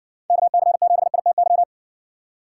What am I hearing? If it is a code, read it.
Morse code, heard as a keyed tone, about 59 wpm, 700 Hz. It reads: KC7ITY